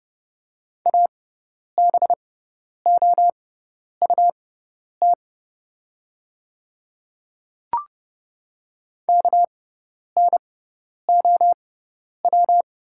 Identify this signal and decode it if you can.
Morse code, heard as a keyed tone, about 30 wpm, spaced as for 10 wpm, 700 Hz. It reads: ABOUT KNOW
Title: 6x Character Spacing - Top 100 Words - 30wpm